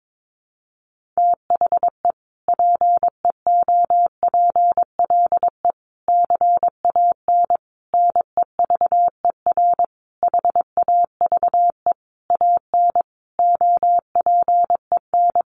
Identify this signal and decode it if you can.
Morse code, heard as a keyed tone, about 22 wpm, 700 Hz. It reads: THE PEOPLE CAN NEVER HAVE AN OPEN
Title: Sentences from Top 1400 Words 22wpm